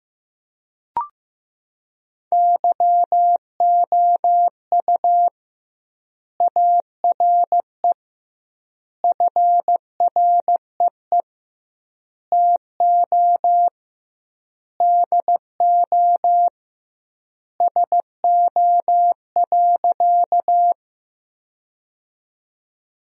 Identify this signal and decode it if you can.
Morse code, heard as a keyed tone, about 15 wpm, 700 Hz. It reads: YOU ARE FREE TO DO SO.
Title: 2x Word-Spacing - Sentences from Top 500 Words - 15wpm